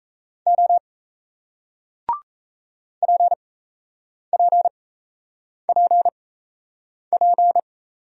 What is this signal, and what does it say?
Morse code, heard as a keyed tone, about 38 wpm, spaced as for 8 wpm, 700 Hz. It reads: O PPPP